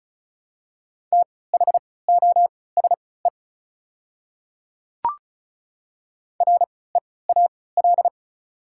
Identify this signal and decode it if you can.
Morse code, heard as a keyed tone, about 36 wpm, spaced as for 20 wpm, 700 Hz. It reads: THOSE REAL
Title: 3x Character Spacing - Top 300 Words - 35wpm